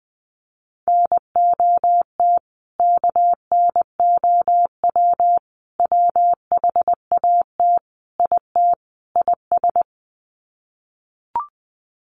Morse code, heard as a keyed tone, about 20 wpm, 700 Hz. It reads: NOT KNOW WHAT IT IS
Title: Sets of 5 Words Encore 20wpm